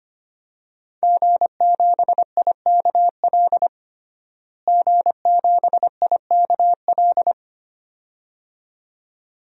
Morse code, heard as a keyed tone, about 25 wpm, 700 Hz. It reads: G7IKL G7IKL
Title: DXCC-100 Call Signs - 5 Characters - Repeated 2x - 25wpm